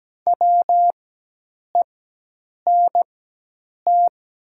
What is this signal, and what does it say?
Morse code, heard as a keyed tone, about 17 wpm, spaced as for 8 wpm, 700 Hz. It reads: WENT
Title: 4x Character Spacing - Top 200 Words - 17wpm